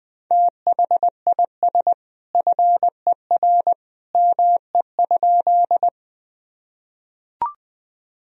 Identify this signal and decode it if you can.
Morse code, heard as a keyed tone, about 20 wpm, 700 Hz. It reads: THIS FER ME?
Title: Sentences from Top 200 Words and Abbreviations 20wpm